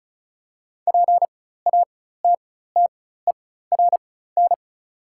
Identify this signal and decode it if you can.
Morse code, heard as a keyed tone, about 36 wpm, spaced as for 16 wpm, 700 Hz. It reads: PATTERN
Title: 4x Character Spacing - Top 500 Words - 35wpm